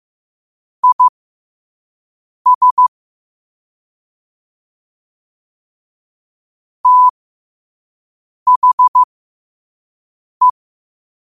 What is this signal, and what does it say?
Morse code, heard as a keyed tone, about 15 wpm, spaced as for 5 wpm, 1000 Hz. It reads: IS THE
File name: qotdfiles / 5wpm